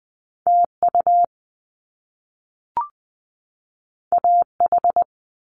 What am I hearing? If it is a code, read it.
Morse code, heard as a keyed tone, about 20 wpm, 700 Hz. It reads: T U A H
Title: Top Two Letter Combinations in Words 20wpm